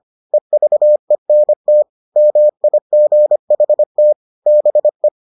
Morse code, heard as a keyed tone, about 25 wpm, 600 Hz. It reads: EVENT MIGHT BE